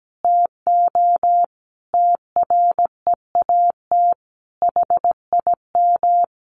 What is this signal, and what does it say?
Morse code, heard as a keyed tone, about 17 wpm, 700 Hz. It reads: TO TREAT HIM
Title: Sentences from Top 1300 Words 17wpm